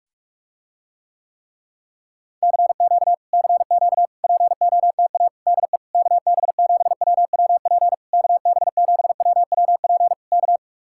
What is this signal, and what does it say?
Morse code, heard as a keyed tone, about 49 wpm, 700 Hz. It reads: CQ CQ POTA DE KB7WWP KB7WWP K